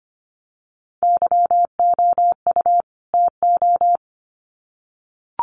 Morse code, heard as a keyed tone, about 25 wpm, 700 Hz. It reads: YOU TO
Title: Sets of 2 Words 25wpm